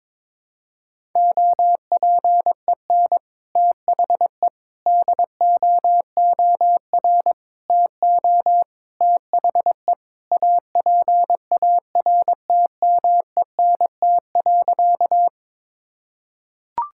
Morse code, heard as a keyed tone, about 21 wpm, 700 Hz. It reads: OPEN THE DOOR TO THE APARTMENT.
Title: Sentences from Top 1000 Words - Review Part 6 - 22wpm